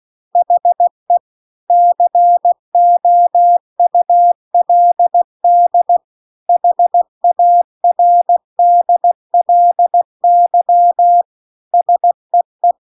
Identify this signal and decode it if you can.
Morse code, heard as a keyed tone, about 16 wpm, 700 Hz. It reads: HE COULD HARDLY SEE